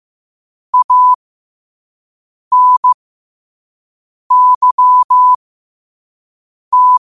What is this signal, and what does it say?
Morse code, heard as a keyed tone, about 15 wpm, spaced as for 5 wpm, 1000 Hz. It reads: ANYT
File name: qotdfiles / 5wpm